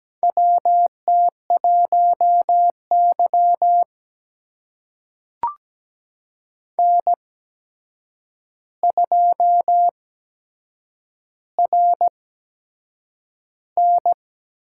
Morse code, heard as a keyed tone, about 17 wpm, spaced as for 7 wpm, 700 Hz. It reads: WT1Y N2RN
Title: USA Call Signs - 4 Characters - 8x Character Spacing - 17wpm